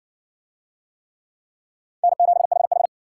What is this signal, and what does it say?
Morse code, heard as a keyed tone, about 60 wpm, 700 Hz. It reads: N7HH